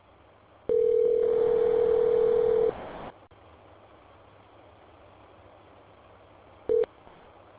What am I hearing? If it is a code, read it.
Ringback tone (North America), 440 + 480 Hz.